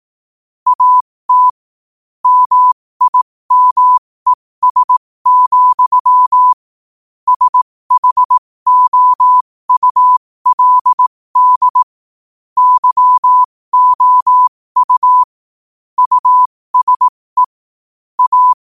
Morse code, heard as a keyed tone, about 18 wpm, spaced as for 15 wpm, 1000 Hz. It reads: AT MIMES, SHOULD YOU USE A